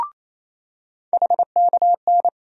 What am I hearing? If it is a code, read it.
Morse code, heard as a keyed tone, about 28 wpm, 700 Hz. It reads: HKN
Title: USA Call Sign Postfixes - 3 Letters 28wpm